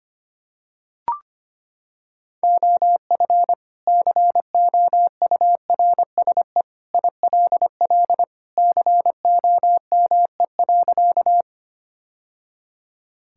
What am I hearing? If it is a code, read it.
Morse code, heard as a keyed tone, about 25 wpm, 700 Hz. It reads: OF COURSE ILL COME.
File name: practice / 25wpm / Sentences with 2-Word Phrases - 25wpm